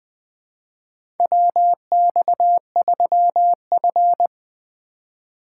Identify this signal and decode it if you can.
Morse code, heard as a keyed tone, about 20 wpm, 700 Hz. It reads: WX3F
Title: USA Call Signs - 4x Character Spacing - 20wpm